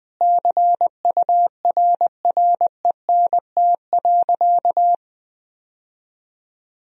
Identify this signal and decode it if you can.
Morse code, heard as a keyed tone, about 21 wpm, 700 Hz. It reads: CURRENT.
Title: Sentences from Top 1000 Words - Review Part 4 - 20wpm